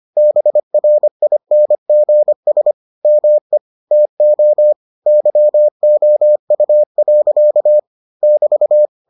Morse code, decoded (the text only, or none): BRINGS ME TO YOU. <BT>